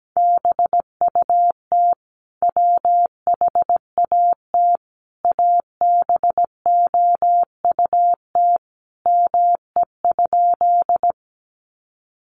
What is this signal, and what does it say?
Morse code, heard as a keyed tone, about 17 wpm, 700 Hz. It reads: BUT WHAT ABOUT ME?
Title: Sentences from Top 200 Words 17wpm